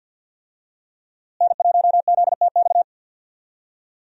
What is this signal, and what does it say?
Morse code, heard as a keyed tone, about 52 wpm, 700 Hz. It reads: N1ZTX
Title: USA Call Signs - 5 Characters - Repeated 3x - 50wpm